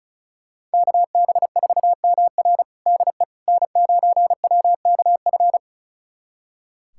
Morse code, heard as a keyed tone, about 36 wpm, 700 Hz. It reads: KB4MR DE N9WKF
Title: USA Callsign DE Callsign - 5 Characters - 35wpm